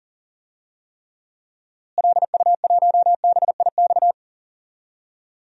Morse code, heard as a keyed tone, about 40 wpm, 700 Hz. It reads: LU1BIX
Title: DXCC-100 Call Signs - 6 Characters - Repeated 3x - 40wpm